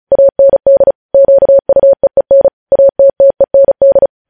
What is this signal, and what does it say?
Morse code, heard as a keyed tone, about 36 wpm, 550 Hz. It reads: AND QUEEN ATTEND
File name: podcast / News headlines-1x-35x35